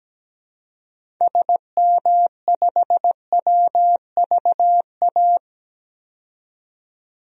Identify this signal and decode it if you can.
Morse code, heard as a keyed tone, about 17 wpm, 700 Hz. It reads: SM5WVA
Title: DXCC-100 Call Signs - 6 Characters - 17wpm